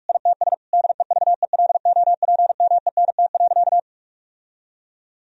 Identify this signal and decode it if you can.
Morse code, heard as a keyed tone, about 46 wpm, 700 Hz. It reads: ITS DEVELOPMENT.
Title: Sentences from Top 1000 Words - Review Part 4 - 45wpm